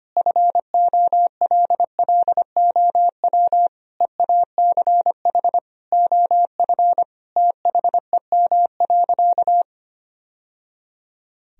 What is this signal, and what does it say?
Morse code, heard as a keyed tone, about 25 wpm, 700 Hz. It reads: FOLLOW EACH OF THEM.
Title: Sentences from Top 1000 Words - Review Part 1 - 25wpm